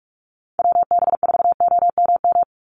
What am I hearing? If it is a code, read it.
Morse code, heard as a keyed tone, about 44 wpm, 700 Hz. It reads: WB4OMM
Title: Call Signs Repeated 3x - 45wpm